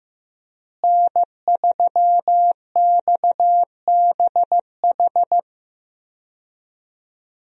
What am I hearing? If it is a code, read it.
Morse code, heard as a keyed tone, about 15 wpm, 700 Hz. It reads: N3XBH
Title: USA Call Signs - 5 Characters - Repeated 2x - 15wpm